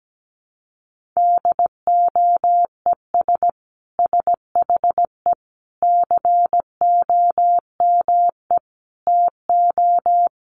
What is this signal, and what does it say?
Morse code, heard as a keyed tone, about 17 wpm, 700 Hz. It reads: DOES SHE COME TO
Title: Sentences from Top 1500 Words 17wpm